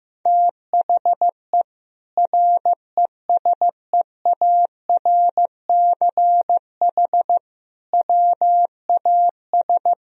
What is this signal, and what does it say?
Morse code, heard as a keyed tone, about 16 wpm, 700 Hz. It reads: THE RESEARCH WAS
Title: Sentences from Top 1000 Words - Review Part 4 - 15wpm